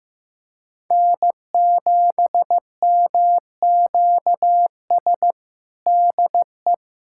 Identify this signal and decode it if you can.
Morse code, heard as a keyed tone, about 16 wpm, 700 Hz. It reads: N7MQS DE
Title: USA Callsign DE Callsign 15wpm